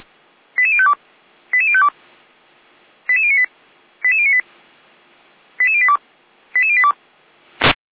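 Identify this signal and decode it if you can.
Five-tone selective call (ZVEI), heard as quick runs of five tones, tones of about 70 ms. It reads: 88052 88052 88098 88098 88082 88082
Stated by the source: ZVEI five-tone call received by SDR.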